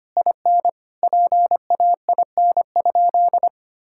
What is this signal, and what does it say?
Morse code, heard as a keyed tone, about 25 wpm, 700 Hz. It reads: IN PAIN?
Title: Sentences from Top 1600 Words 25wpm